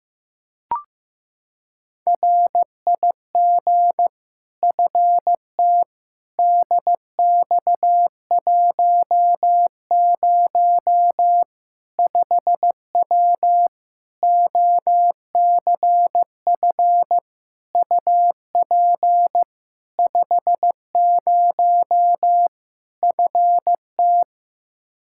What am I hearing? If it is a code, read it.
Morse code, heard as a keyed tone, about 15 wpm, 700 Hz. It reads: RIG FT DX10 5W OCF UP 50 FT
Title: Practice QSO - Exchange 2 - Info - 15wpm